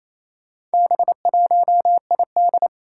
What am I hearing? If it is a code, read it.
Morse code, heard as a keyed tone, about 28 wpm, 700 Hz. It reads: B1ID
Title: Four Challenging Letter-Number Combinations 28wpm